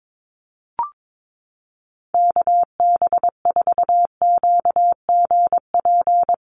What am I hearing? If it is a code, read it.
Morse code, heard as a keyed tone, about 22 wpm, 700 Hz. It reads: KB4QGP
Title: USA Call Signs 22wpm